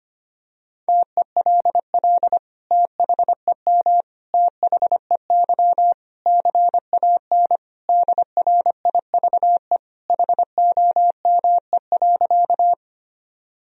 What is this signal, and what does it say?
Morse code, heard as a keyed tone, about 25 wpm, 700 Hz. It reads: TELL THEM THEY CAN DRIVE HOME.
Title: Sentences from Top 1000 Words - Review Part 2 - 25wpm